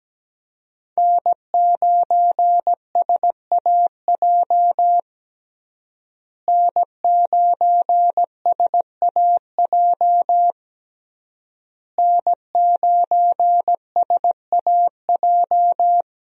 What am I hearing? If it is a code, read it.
Morse code, heard as a keyed tone, about 17 wpm, 700 Hz. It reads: N9SAJ N9SAJ N9SAJ
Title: USA Call Signs - 5 Characters - Repeated 3x - 17wpm